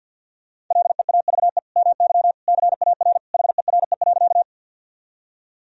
Morse code, heard as a keyed tone, about 52 wpm, 700 Hz. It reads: LEAVE MY CAR HERE.
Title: Sentences from Top 1000 Words - Review Part 1 - 50wpm